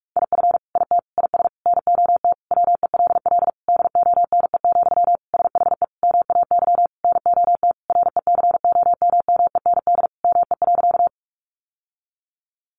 Morse code, heard as a keyed tone, about 44 wpm, 700 Hz. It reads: IF IT IS NOT WELL DONE, SHE MAY NOT RECOMMEND ME.
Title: Sentences from Top 1200 Words 45wpm